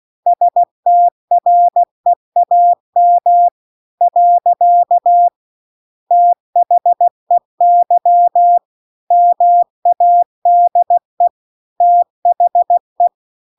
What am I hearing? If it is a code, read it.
Morse code, heard as a keyed tone, about 16 wpm, 700 Hz. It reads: STREAM . THEY MADE THE
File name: 16WPM